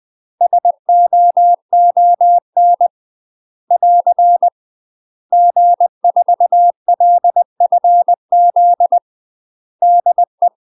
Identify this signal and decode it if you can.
Morse code, heard as a keyed tone, about 20 wpm, 700 Hz. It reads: SOON <AR> G4LFZ DE
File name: UK MASTER QSOs TRAINING / 20WPM / UK MASTER QSOs 0005